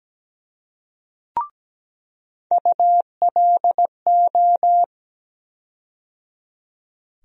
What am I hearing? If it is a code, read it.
Morse code, heard as a keyed tone, about 17 wpm, 700 Hz. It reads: ULO
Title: USA Call Sign Postfixes - 3 Letters 17wpm